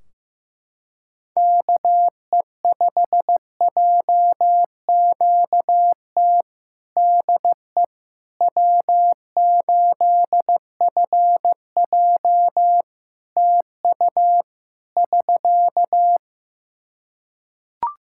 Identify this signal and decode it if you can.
Morse code, heard as a keyed tone, about 15 wpm, 700 Hz. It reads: KE5JQT DE W8FJ TU <SK>
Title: Practice QSO - Exchange 4 - Ending - 15wpm